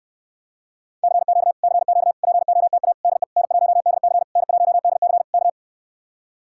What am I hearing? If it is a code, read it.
Morse code, heard as a keyed tone, about 69 wpm, 700 Hz. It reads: CQ CQ POTA DE N1DC N1DC K